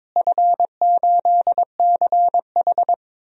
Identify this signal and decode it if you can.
Morse code, heard as a keyed tone, about 22 wpm, 700 Hz. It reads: F8CH
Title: DXCC-100 Call Signs - 4 Characters - Repeated 3x - 22wpm